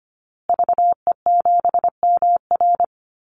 Morse code, heard as a keyed tone, about 25 wpm, 700 Hz. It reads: VE7MR
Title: Call Signs Repeated 2x - 25wpm